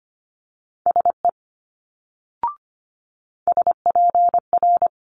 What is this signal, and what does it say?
Morse code, heard as a keyed tone, about 25 wpm, 700 Hz. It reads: SE SPR
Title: Multi-letter Phonemes Quick Fire 25wpm